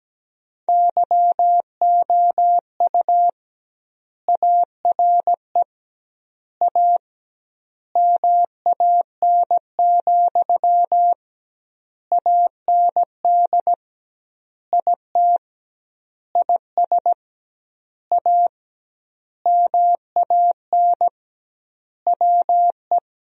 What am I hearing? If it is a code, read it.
Morse code, heard as a keyed tone, about 17 wpm, 700 Hz. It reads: YOU ARE A MAN, AND IT IS A MAN WE
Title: 2x Word-Spacing - Sentences from Top 200 Words - 17wpm